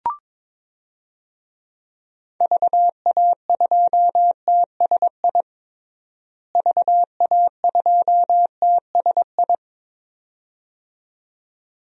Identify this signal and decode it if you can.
Morse code, heard as a keyed tone, about 21 wpm, 700 Hz. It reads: VA2TSI VA2TSI
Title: DXCC-100 Call Signs - 6 Characters - Repeated 2x - 22wpm